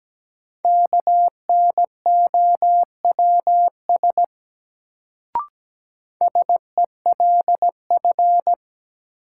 Morse code, heard as a keyed tone, about 17 wpm, 700 Hz. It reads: KNOWS SELF
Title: Sentences from Top 1000 Words - Review Part 1 - New Word Drill - 17wpm